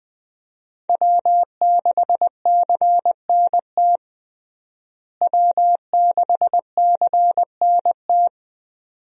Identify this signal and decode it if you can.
Morse code, heard as a keyed tone, about 20 wpm, 700 Hz. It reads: W6CNT W6CNT
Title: USA Call Signs - 5 Characters - Repeated 2x - 20wpm